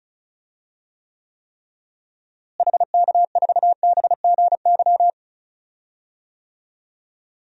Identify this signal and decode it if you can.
Morse code, heard as a keyed tone, about 36 wpm, 700 Hz. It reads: HK4BGY
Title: DXCC-100 Call Signs - 6 Characters - Repeated 3x - 35wpm